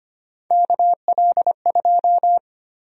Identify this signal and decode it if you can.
Morse code, heard as a keyed tone, about 25 wpm, 700 Hz. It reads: KL2
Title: USA Call Sign Prefixes - 25wpm